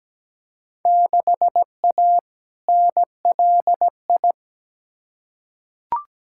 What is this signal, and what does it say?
Morse code, heard as a keyed tone, about 17 wpm, 700 Hz. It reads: 6A NLI
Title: Field Day Exchanges Encore - 17wpm